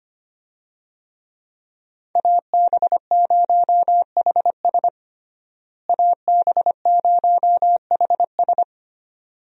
Morse code, heard as a keyed tone, about 25 wpm, 700 Hz. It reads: AB0HS AB0HS
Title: USA Call Signs - 5 Characters - Repeated 2x - 25wpm